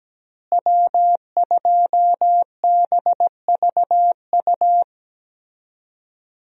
Morse code, heard as a keyed tone, about 17 wpm, 700 Hz. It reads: W2BVU
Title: USA Call Signs - 5 Characters - Repeated 2x - 17wpm